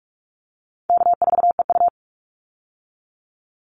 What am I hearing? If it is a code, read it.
Morse code, heard as a keyed tone, about 44 wpm, 700 Hz. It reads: K4EU
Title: Easy Call Signs - 45wpm